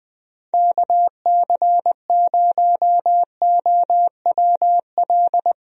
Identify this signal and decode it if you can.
Morse code, heard as a keyed tone, about 20 wpm, 700 Hz. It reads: KC0OWL
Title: USA Call Signs - 6 Characters - Repeated 3x - 20wpm